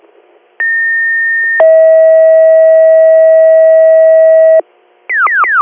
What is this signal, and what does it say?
Two-tone page, heard as a one-second tone, then a three-second tone, about 1800 then 640 Hz. It is heard from 0.6 to 4.6 s.